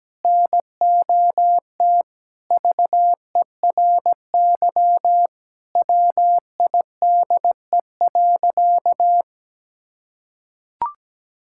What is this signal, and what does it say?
Morse code, heard as a keyed tone, about 17 wpm, 700 Hz. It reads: NOT VERY WIDE.
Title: Sentences from Top 1000 Words - Review Part 6 - 17wpm